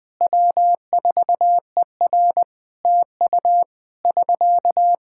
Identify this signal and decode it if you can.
Morse code, heard as a keyed tone, about 20 wpm, 700 Hz. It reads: W4ER TU <SK>
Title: Practice QSO - Exchange 4 - Ending - 20wpm